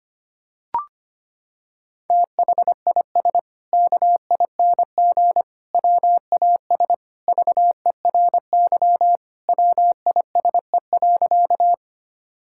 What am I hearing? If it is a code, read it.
Morse code, heard as a keyed tone, about 25 wpm, 700 Hz. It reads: THIS KING WAS VERY WISE.
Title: Sentences from Top 1600 Words 25wpm